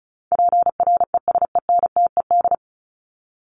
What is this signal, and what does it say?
Morse code, heard as a keyed tone, about 34 wpm, 700 Hz. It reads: PRESENTED
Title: ED Suffix - Top 100 Words with Base Words 35wpm